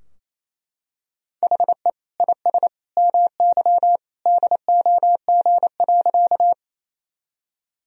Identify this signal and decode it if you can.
Morse code, heard as a keyed tone, about 28 wpm, 700 Hz. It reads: HE IS MY DOG.
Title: Sentences from Top 1000 Words - Review Part 2 - 28wpm